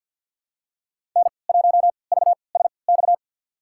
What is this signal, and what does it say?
Morse code, heard as a keyed tone, about 47 wpm, spaced as for 28 wpm, 700 Hz. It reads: N1VSX
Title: USA Call Signs - 3x Character Spacing - 50wpm